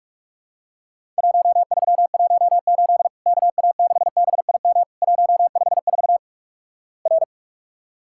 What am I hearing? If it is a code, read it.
Morse code, heard as a keyed tone, about 46 wpm, 700 Hz. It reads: 1318 KA6BIM 154 R
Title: Worked All Europe DX Contest - QTC Exchange - 45wpm